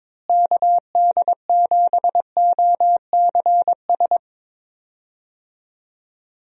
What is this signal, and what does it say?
Morse code, heard as a keyed tone, about 22 wpm, 700 Hz. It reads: KD7OCS
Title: USA Call Signs - 6 Characters - Repeated 3x - 22wpm